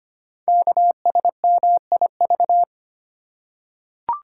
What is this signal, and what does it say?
Morse code, heard as a keyed tone, about 25 wpm, 700 Hz. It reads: KSMIV
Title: Code Groups - 5 Letters - 25wpm